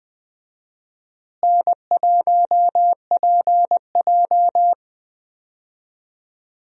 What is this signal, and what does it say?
Morse code, heard as a keyed tone, about 21 wpm, 700 Hz. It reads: N1PJ
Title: USA Call Signs - 4 Characters - Repeated 2x - 20wpm